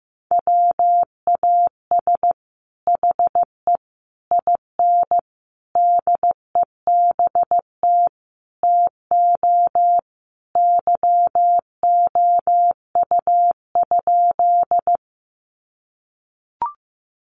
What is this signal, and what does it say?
Morse code, heard as a keyed tone, about 15 wpm, 700 Hz. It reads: WAS HE IN DEBT TO YOU?
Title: Sentences from Top 1300 Words 15wpm